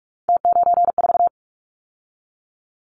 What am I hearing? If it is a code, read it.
Morse code, heard as a keyed tone, about 46 wpm, 700 Hz. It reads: T94
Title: Callsign Prefixes for Top 50 DXCC - 45wpm